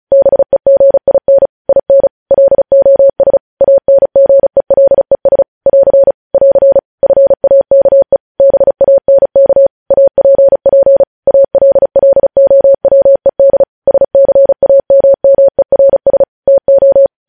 Morse code, heard as a keyed tone, about 36 wpm, 550 Hz. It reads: BEGIN IN LOS ANGELES <AR> <AR> FAKE BANK APP ALLOWED SCAMMERS TO
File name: podcast / News headlines-1x-35x35